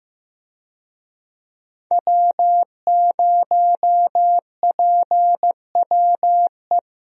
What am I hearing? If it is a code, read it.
Morse code, heard as a keyed tone, about 15 wpm, 700 Hz. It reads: W0PWE
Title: USA Call Signs - 5 Characters - Repeated 3x - 15wpm